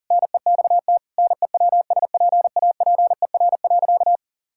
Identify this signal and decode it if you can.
Morse code, heard as a keyed tone, about 40 wpm, 700 Hz. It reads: NEXT NEWSPAPER.
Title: Sentences from Top 1000 Words - Review Part 6 - 40wpm